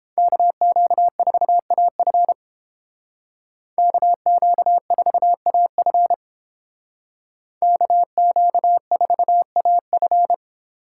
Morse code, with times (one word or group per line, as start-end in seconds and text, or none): 0.2-2.3 KQ4AF
3.8-6.1 KQ4AF
7.6-10.3 KQ4AF